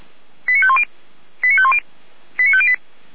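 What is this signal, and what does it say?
Five-tone selective call (ZVEI), heard as quick runs of five tones, tones of about 70 ms. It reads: 89510 89510 89508
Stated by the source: Received by SDR, ZVEI five-tone call 89310.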